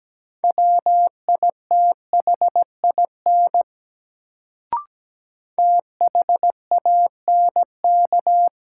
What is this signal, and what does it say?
Morse code, heard as a keyed tone, about 17 wpm, 700 Hz. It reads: WITHIN THANK
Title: Sets of 2 Words - New Word Drill - 17wpm